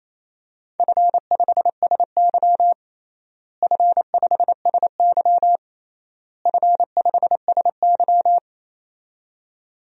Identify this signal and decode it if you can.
Morse code, heard as a keyed tone, about 28 wpm, 700 Hz. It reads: F5SY F5SY F5SY
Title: DXCC-100 Call Signs - 4 Characters - Repeated 3x - 28wpm